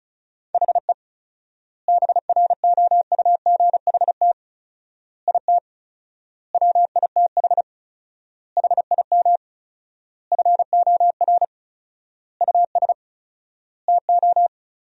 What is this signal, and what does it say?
Morse code, heard as a keyed tone, about 36 wpm, 700 Hz. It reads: HE BROUGHT IT WITH HIM FOR US TO
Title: 4x Word-Spacing - Sentences from Top 500 Words - 35wpm